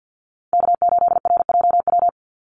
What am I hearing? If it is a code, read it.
Morse code, heard as a keyed tone, about 50 wpm, 700 Hz. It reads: K8RJW
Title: Call Signs - 50wpm